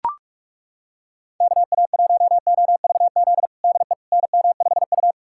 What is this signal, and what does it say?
Morse code, heard as a keyed tone, about 43 wpm, 700 Hz. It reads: KA1OVZ DE NM5U
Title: USA Callsign DE Callsign 45wpm